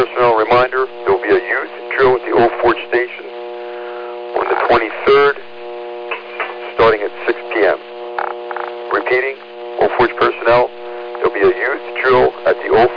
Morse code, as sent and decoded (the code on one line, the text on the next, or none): none